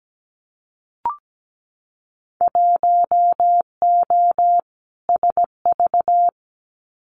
Morse code, heard as a keyed tone, about 17 wpm, 700 Hz. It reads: 1O SV
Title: Winter Field Day Exchanges 17wpm